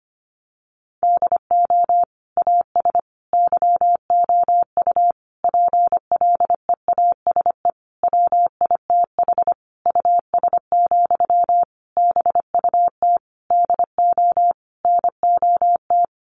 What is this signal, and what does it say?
Morse code, heard as a keyed tone, about 25 wpm, 700 Hz. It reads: DO AS YOU PLEASE WITH US, BUT DO NOT
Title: Sentences from Top 1500 Words 25wpm